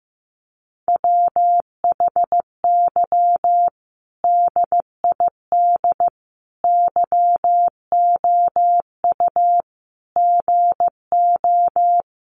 Morse code, heard as a keyed tone, about 15 wpm, 700 Hz. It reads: WHY DID YOU GO
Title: Sentences from Top 1100 Words 15wpm